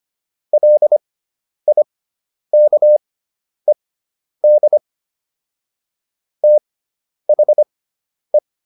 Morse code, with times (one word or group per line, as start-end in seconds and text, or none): 0.5-4.8 LIKED
6.4-8.4 THE